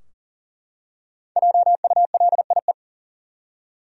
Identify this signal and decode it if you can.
Morse code, heard as a keyed tone, about 40 wpm, 700 Hz. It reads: JULIE
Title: USA Common Names - 40wpm